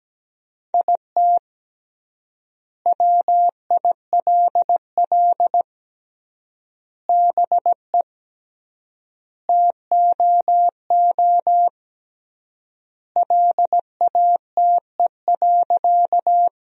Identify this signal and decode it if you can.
Morse code, heard as a keyed tone, about 17 wpm, 700 Hz. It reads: IT WILL BE TOO LATE.
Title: 3x Word-Spacing - Sentences from Top 300 Words - 17wpm